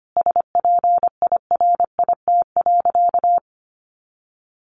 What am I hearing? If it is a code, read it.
Morse code, heard as a keyed tone, about 25 wpm, 700 Hz. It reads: SPIRIT.
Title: Sentences from Top 1400 Words 25wpm